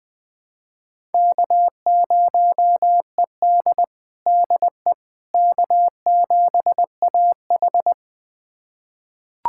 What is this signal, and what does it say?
Morse code, heard as a keyed tone, about 20 wpm, 700 Hz. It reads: K0ED DE K7AH
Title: USA Callsign DE Callsign - 4 Characters - 20wpm